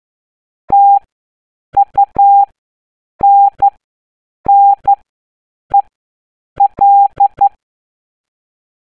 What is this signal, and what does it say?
Morse code, heard as a keyed tone, about 12 wpm, spaced as for 7 wpm, 800 Hz. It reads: TUNNEL
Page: ascolta e risolvi il messaggio segreto in codice Morse per scoprire il prossimo nascondiglio.